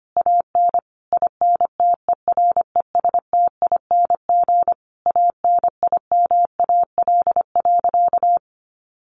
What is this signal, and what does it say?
Morse code, heard as a keyed tone, about 25 wpm, 700 Hz. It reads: AN INTERESTING ANIMAL.
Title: Easy Phrases 25wpm